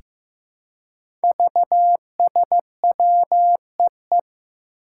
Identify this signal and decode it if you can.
Morse code, heard as a keyed tone, about 15 wpm, 700 Hz. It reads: VSWEE
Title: Five Challenging Letter-Number Combinations 15wpm